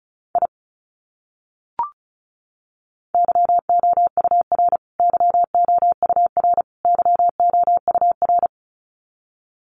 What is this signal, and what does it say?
Morse code, heard as a keyed tone, about 34 wpm, 700 Hz. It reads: I YOUR YOUR YOUR
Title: Top 200 Words Repeated 35wpm